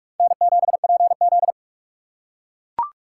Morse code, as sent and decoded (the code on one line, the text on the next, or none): -. --... .--. --..
N7PZ